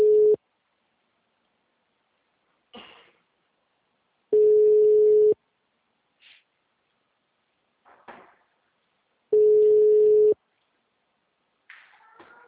Ringback tone (Europe, 425 Hz).